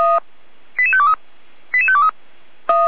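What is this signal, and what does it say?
Five-tone selective call (ZVEI), heard as quick runs of five tones, tones of about 70 ms. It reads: 88523 88523
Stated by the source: ZVEI five-tone call received on an SDR.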